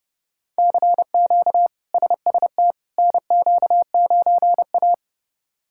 Morse code, heard as a keyed tone, about 30 wpm, 700 Hz. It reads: CQ SST NQ9A
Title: K1USN SST QSO Practice - 30wpm